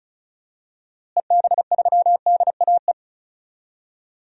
Morse code, heard as a keyed tone, about 34 wpm, 700 Hz. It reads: EB3DAE
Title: DXCC-100 Call Signs - 6 Characters - Repeated 3x - 35wpm